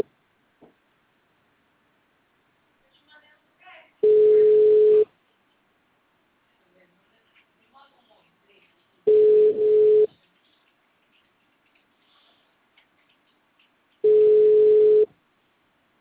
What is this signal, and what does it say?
Ringback tone (Europe, 425 Hz), with speech.